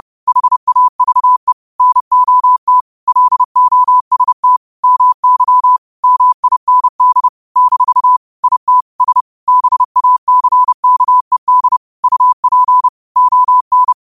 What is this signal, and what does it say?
Morse code, heard as a keyed tone, about 30 wpm, 1000 Hz. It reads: HAVE NOT LOST MY MIND - IT S BACKED UP ON
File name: qotdfiles / 30wpm